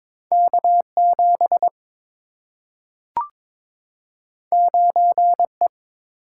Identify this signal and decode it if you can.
Morse code, heard as a keyed tone, about 22 wpm, 700 Hz. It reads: K7 9E
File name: practice / 22wpm / Two Letter-Number Combinations - 22wpm